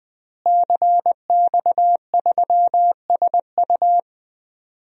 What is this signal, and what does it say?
Morse code, heard as a keyed tone, about 21 wpm, 700 Hz. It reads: CX3SU
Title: DXCC-100 Call Signs - 5 Characters - Repeated 3x - 20wpm